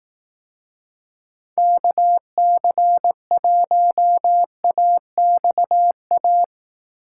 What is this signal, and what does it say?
Morse code, heard as a keyed tone, about 18 wpm, 700 Hz. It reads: KC1AXA
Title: Speed Racing - USA 2x3 Call Signs 15wpm